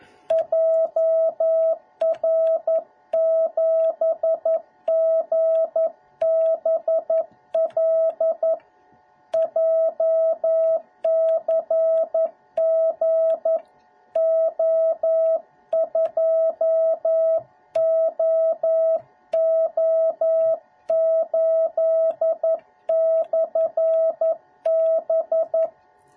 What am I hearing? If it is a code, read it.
Morse code, heard as a keyed tone, about 11 wpm, 650 Hz. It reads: JR7GBL JCGO2OO8/B